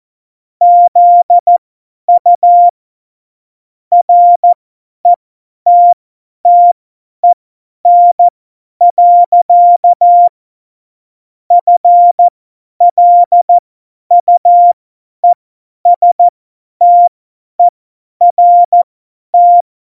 Morse code, heard as a keyed tone, about 14 wpm, spaced as for 10 wpm, 700 Hz. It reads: ZU RETTEN. FLUESTERT
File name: Morsekurs / Witze / Witz 11 Juli 17, Tempo 50 F70